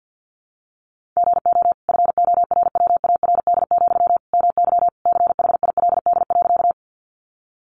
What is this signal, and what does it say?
Morse code, heard as a keyed tone, about 48 wpm, 700 Hz. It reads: GO FORWARD, MY CHILD.
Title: Sentences from Top 1300 Words 50wpm